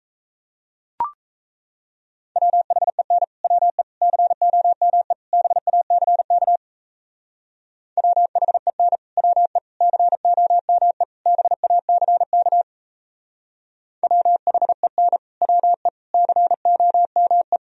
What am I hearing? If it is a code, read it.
Morse code, heard as a keyed tone, about 34 wpm, 700 Hz. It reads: WHEN WE COME BACK WHEN WE COME BACK WHEN WE COME
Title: Speed-Racing - Sets of 4 Words 28wpm